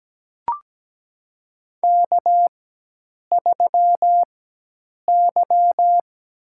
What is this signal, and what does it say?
Morse code, heard as a keyed tone, about 17 wpm, spaced as for 8 wpm, 700 Hz. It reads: K3Y
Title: USA Call Signs - 5 Characters - 4x Character Spacing - 17wpm